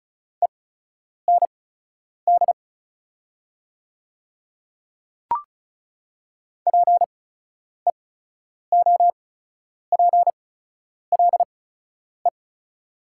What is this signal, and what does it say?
Morse code, heard as a keyed tone, about 34 wpm, spaced as for 10 wpm, 700 Hz. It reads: END PEOPLE
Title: 8x Character Spacing - Top 300 Words - 35wpm